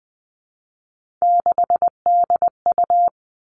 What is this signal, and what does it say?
Morse code, heard as a keyed tone, about 20 wpm, 700 Hz. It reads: 6DU
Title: Three Challenging Letter-Number Combinations 20wpm